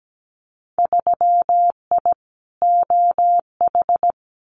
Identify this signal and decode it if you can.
Morse code, heard as a keyed tone, about 17 wpm, 700 Hz. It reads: 3I OH
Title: Winter Field Day Exchanges Encore 17wpm